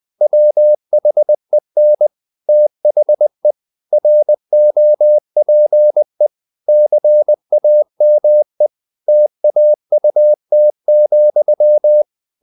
Morse code, heard as a keyed tone, about 20 wpm, 600 Hz. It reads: WHEN THE ROPE CAME TAUT,
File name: the old man and the sea part 2 20wpm